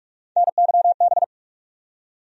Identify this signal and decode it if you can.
Morse code, heard as a keyed tone, about 46 wpm, 700 Hz. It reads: NYB
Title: USA Call Sign Postfixes - 3 Letters 45wpm